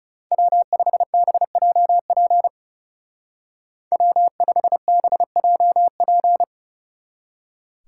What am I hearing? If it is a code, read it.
Morse code, heard as a keyed tone, about 37 wpm, 700 Hz. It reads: W5BJP W5BJP
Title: Speed Racing - USA 1x3 Call Signs 30wpm